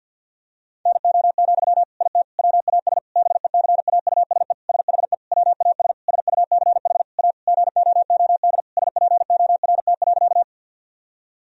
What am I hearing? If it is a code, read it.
Morse code, heard as a keyed tone, about 50 wpm, 700 Hz. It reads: NO, IT WAS BECAUSE SHE WAS SUCH A GOOD SPORT.
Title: Sentences from Top 1000 Words - Review Part 4 - 50wpm